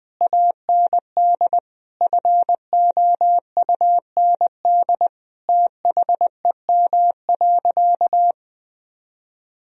Morse code, heard as a keyed tone, about 20 wpm, 700 Hz. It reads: AND FOUND THEM.
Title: Sentences from Top 1000 Words - Review Part 1 - 20wpm